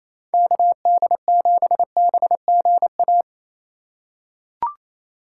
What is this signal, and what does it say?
Morse code, heard as a keyed tone, about 28 wpm, 700 Hz. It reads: KD7BGA
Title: USA Call Signs - 6 Characters - 2x Character Spacing - 28wpm